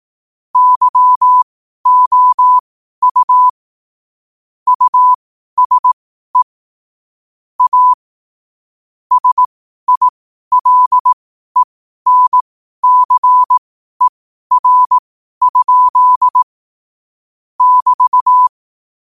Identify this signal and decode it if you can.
Morse code, heard as a keyed tone, about 18 wpm, spaced as for 12 wpm, 1000 Hz. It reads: YOU USE A SILENCER? <BT>
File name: qotdfiles / 12wpm